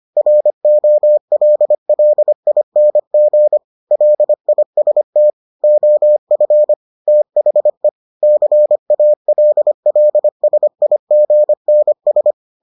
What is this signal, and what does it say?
Morse code, heard as a keyed tone, about 25 wpm, 600 Hz. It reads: ROLLING LIST OF THE CALLSIGNS